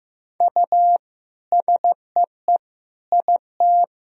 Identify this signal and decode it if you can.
Morse code, heard as a keyed tone, about 15 wpm, 700 Hz. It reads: U SEE IT
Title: Sentences from Top 200 Words and Abbreviations 15wpm